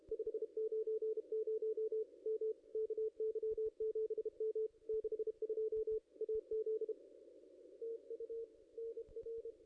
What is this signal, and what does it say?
Morse code, heard as a keyed tone, about 32 wpm, 440 Hz. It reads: 590 M KY7M 62 AZ TU NR
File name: contests / ss-cw-k5tr-op-n5rz -0400